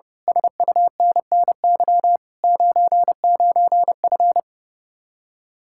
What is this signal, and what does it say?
Morse code, heard as a keyed tone, about 28 wpm, 700 Hz. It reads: SUNNY 99F
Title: Speed-Racing Common QSO Phrases Encore - 30wpm